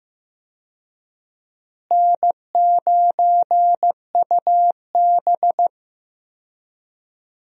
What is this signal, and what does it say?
Morse code, heard as a keyed tone, about 15 wpm, 700 Hz. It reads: N9UB
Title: USA Call Signs - 4 Characters - Repeated 3x - 15wpm